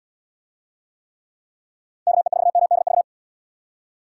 Morse code, heard as a keyed tone, about 77 wpm, 700 Hz. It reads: KE4MDV